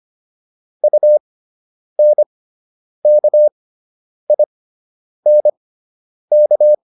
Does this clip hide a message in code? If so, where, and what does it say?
Yes: Morse code from 0.8 to 6.7 s, heard as a keyed tone, about 25 wpm, spaced as for 9 wpm, 600 Hz. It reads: UNKINK